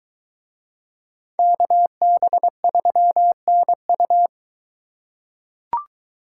Morse code, heard as a keyed tone, about 21 wpm, 700 Hz. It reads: KB3NU